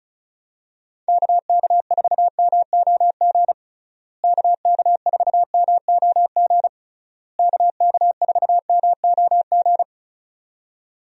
Morse code, heard as a keyed tone, about 36 wpm, 700 Hz. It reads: KK4MOG KK4MOG KK4MOG
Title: USA Call Signs - 6 Characters - Repeated 3x - 35wpm